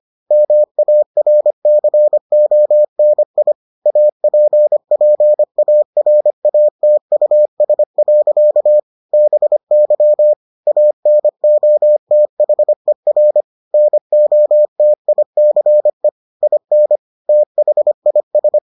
Morse code, heard as a keyed tone, about 25 wpm, 600 Hz. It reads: MARCONI APPARATUS. BY ANOTHER NOTICE IN THIS